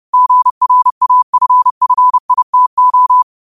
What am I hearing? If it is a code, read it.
Morse code, heard as a keyed tone, about 30 wpm, 1000 Hz. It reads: GRAFFITO